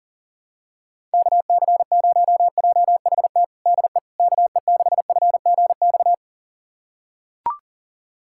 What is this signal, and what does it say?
Morse code, heard as a keyed tone, about 40 wpm, 700 Hz. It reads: KC0JHT DE KE6FGX
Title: USA Callsign DE Callsign - 6 Characters - 40wpm